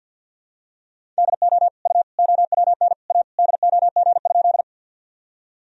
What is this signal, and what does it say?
Morse code, heard as a keyed tone, about 52 wpm, 700 Hz. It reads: DO U OWN A DOG?
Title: Sentences from Top 400 Words and Abbreviations 50wpm